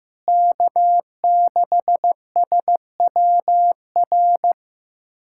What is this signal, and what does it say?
Morse code, heard as a keyed tone, about 16 wpm, 700 Hz. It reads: K6SWR
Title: USA Callsign DE Callsign - 5 Characters - 15wpm